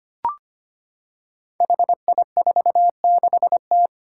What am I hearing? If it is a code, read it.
Morse code, heard as a keyed tone, about 25 wpm, 700 Hz. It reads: HI46T
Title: Five Challenging Letter-Number Combinations 25wpm